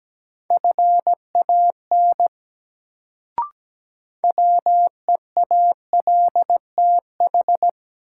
Morse code, heard as a keyed tone, about 17 wpm, 700 Hz. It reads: FAN WEALTH
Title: Sentences from Top 1000 Words - Review Part 4 - New Word Drill - 17wpm